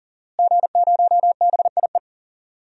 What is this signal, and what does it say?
Morse code, heard as a keyed tone, about 40 wpm, 700 Hz. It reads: G0BIE
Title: DXCC-100 Call Signs - 5 Characters - Repeated 3x - 40wpm